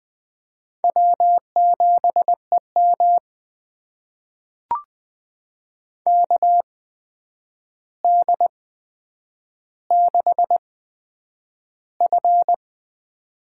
Morse code, heard as a keyed tone, about 20 wpm, spaced as for 9 wpm, 700 Hz. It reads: W7EM KD6F
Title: USA Call Signs - 4 Characters - 8x Character Spacing - 20wpm